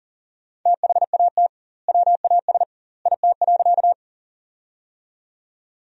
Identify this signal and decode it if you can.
Morse code, heard as a keyed tone, about 40 wpm, 700 Hz. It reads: THAT WAS IT.
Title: 2x Word-Spacing - Sentences from Top 100 Words - 40wpm